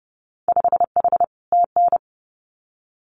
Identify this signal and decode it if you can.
Morse code, heard as a keyed tone, about 30 wpm, 700 Hz. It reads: 5H TN
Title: Winter Field Day Exchanges Encore 30wpm